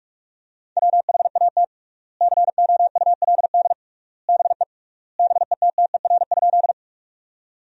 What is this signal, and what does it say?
Morse code, heard as a keyed tone, about 46 wpm, 700 Hz. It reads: WHAT COULD BE BETTER?
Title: 3x Word-Spacing - Sentences from Top 400 Words - 45wpm